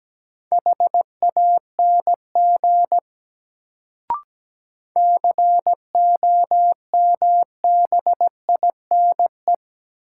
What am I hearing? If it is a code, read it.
Morse code, heard as a keyed tone, about 17 wpm, 700 Hz. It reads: HANG COMBINE
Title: Sentences from Top 1600 Words - New Word Drill - 17wpm